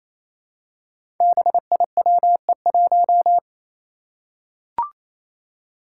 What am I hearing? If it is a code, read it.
Morse code, heard as a keyed tone, about 29 wpm, 700 Hz. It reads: BIWE1
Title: Five Challenging Letter-Number Combinations 28wpm